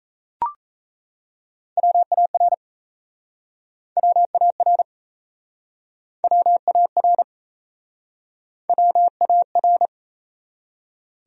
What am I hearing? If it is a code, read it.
Morse code, heard as a keyed tone, about 38 wpm, 700 Hz. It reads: WAR WAR WAR WAR